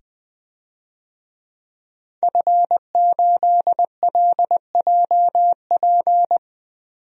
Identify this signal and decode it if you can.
Morse code, heard as a keyed tone, about 20 wpm, 700 Hz. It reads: F8LJP
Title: DXCC-100 Call Signs - 5 Characters - 20wpm